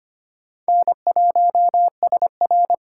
Morse code, heard as a keyed tone, about 31 wpm, 700 Hz. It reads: N1SR
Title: Speed Racing - USA 1x2 Call Signs 25wpm